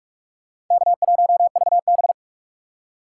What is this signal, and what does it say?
Morse code, heard as a keyed tone, about 44 wpm, 700 Hz. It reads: K1VB